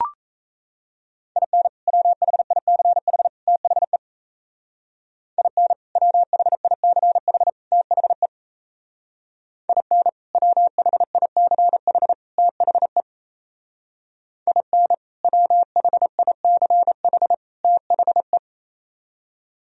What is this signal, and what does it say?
Morse code, heard as a keyed tone, about 36 wpm, 700 Hz. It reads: IN WHICH THE IN WHICH THE IN WHICH THE IN WHICH THE